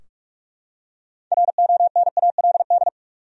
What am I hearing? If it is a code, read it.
Morse code, heard as a keyed tone, about 43 wpm, 700 Hz. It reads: RONALD
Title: USA Common Names - 45wpm